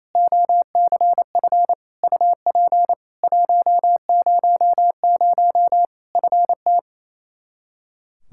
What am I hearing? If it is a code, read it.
Morse code, heard as a keyed tone, about 28 wpm, 700 Hz. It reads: OCF UP 100 FT
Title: Practice QSO - Exchange 2 - Info - 28wpm